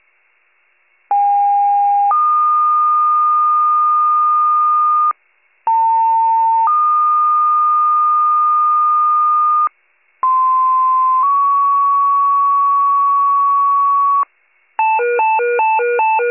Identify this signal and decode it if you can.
Two-tone page, heard as a one-second tone, then a three-second tone, about 810 then 1220 Hz, about 900 then 1220 Hz, about 1040 then 1120 Hz. It is heard from 1.1 to 14.2 s.